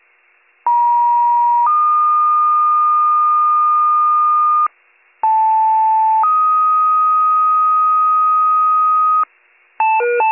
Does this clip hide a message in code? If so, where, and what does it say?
Yes: two-tone page from 0.7 to 9.2 s, heard as a one-second tone, then a three-second tone, about 950 then 1220 Hz, about 880 then 1250 Hz.